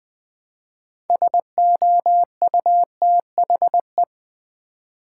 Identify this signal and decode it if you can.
Morse code, heard as a keyed tone, about 20 wpm, 700 Hz. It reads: SOUTHE
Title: Build A Word - 3 Letters At A Time - 8-Letter Words - 20wpm